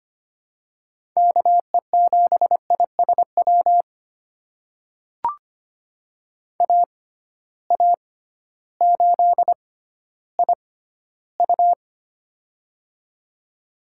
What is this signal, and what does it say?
Morse code, heard as a keyed tone, about 25 wpm, spaced as for 14 wpm, 700 Hz. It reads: KE7ISW AA8IU
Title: USA Call Signs - 6x Character Spacing - 25wpm